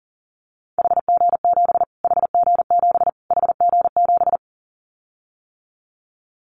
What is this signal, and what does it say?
Morse code, heard as a keyed tone, about 40 wpm, 700 Hz. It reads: HG7 HG7 HG7
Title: Callsign Prefixes for Top 25 DXCC - Repeated 3x - 40wpm